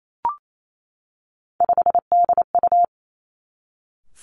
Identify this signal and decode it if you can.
Morse code, heard as a keyed tone, about 28 wpm, 700 Hz. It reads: 5DU
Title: Three Challenging Letter-Number Combinations 28wpm